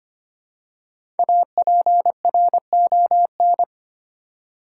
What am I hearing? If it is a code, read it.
Morse code, heard as a keyed tone, about 26 wpm, 700 Hz. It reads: APRON
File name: practice / 25wpm / Easy Words with Hints - 25wpm